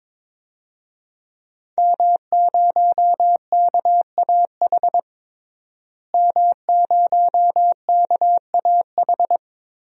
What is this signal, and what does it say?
Morse code, heard as a keyed tone, about 22 wpm, 700 Hz. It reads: M0KAH M0KAH
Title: DXCC-100 Call Signs - 5 Characters - Repeated 3x - 22wpm